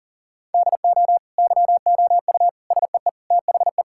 Morse code, heard as a keyed tone, about 40 wpm, 700 Hz. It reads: DO YOU SEE THE
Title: Sentences from Top 1600 Words 40wpm